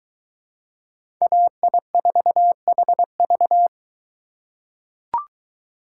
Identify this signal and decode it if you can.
Morse code, heard as a keyed tone, about 20 wpm, 700 Hz. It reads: AI4HV